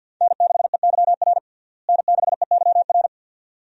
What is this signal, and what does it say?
Morse code, heard as a keyed tone, about 49 wpm, 700 Hz. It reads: N6EYR N6EYR
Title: USA Call Signs - 5 Characters - Repeated 2x - 50wpm